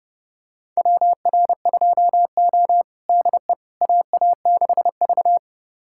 Morse code, heard as a keyed tone, about 30 wpm, 700 Hz. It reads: WR2O DE AA6V
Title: USA Callsign DE Callsign - 4 Characters - 30wpm